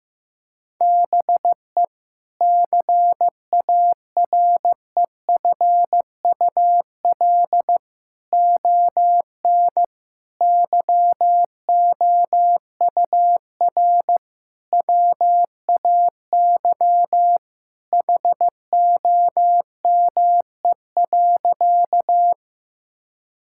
Morse code, heard as a keyed tone, about 15 wpm, 700 Hz. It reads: BE CAREFUL ON YOUR WAY HOME.
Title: Sentences from Top 1600 Words 15wpm